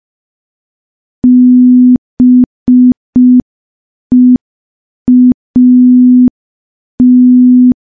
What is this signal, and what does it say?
Morse code, heard as a keyed tone, about 5 wpm, 250 Hz. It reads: BEAT
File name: morse / morse-training-05WPM